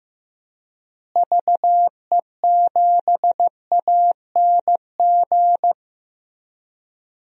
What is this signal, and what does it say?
Morse code, heard as a keyed tone, about 15 wpm, 700 Hz. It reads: VE7ANG
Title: DXCC-100 Call Signs - 6 Characters - Repeated 3x - 15wpm